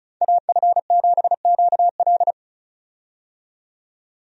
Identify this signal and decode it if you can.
Morse code, heard as a keyed tone, about 36 wpm, 700 Hz. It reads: AF7QL
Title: USA Call Signs - 5 Characters - 35wpm